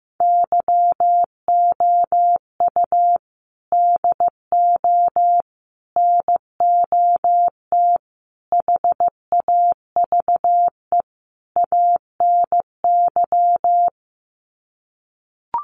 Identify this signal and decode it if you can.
Morse code, heard as a keyed tone, about 15 wpm, 700 Hz. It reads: YOU DO NOT HAVE ANY
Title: Sets of 5 Words Encore 15wpm